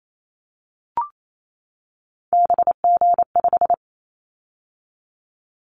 Morse code, heard as a keyed tone, about 28 wpm, 700 Hz. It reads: BG5
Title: Callsign Prefixes for Top 100 DXCC - 28wpm